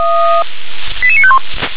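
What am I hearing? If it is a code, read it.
Five-tone selective call (ZVEI), heard as quick runs of five tones, tones of about 70 ms. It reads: 88951
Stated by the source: ZVEI five-tone call 88920, received on SDR.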